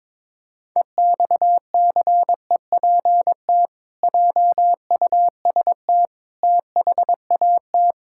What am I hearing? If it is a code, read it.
Morse code, heard as a keyed tone, about 22 wpm, 700 Hz. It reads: EXCEPT JUST THAT
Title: Sentences from Top 1600 Words 22wpm